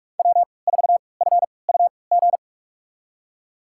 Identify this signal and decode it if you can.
Morse code, heard as a keyed tone, about 46 wpm, spaced as for 25 wpm, 700 Hz. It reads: W4FUG
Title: USA Call Signs - 5 Characters - 3x Character Spacing - 45wpm